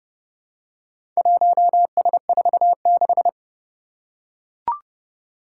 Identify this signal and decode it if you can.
Morse code, heard as a keyed tone, about 32 wpm, 700 Hz. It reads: 1S46
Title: Four Challenging Letter-Number Combinations 30wpm